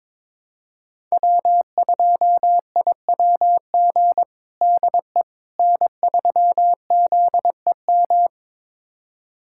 Morse code, heard as a keyed tone, about 22 wpm, 700 Hz. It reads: W2IWG DE N3ZEM
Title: USA Callsign DE Callsign - 5 Characters - 22wpm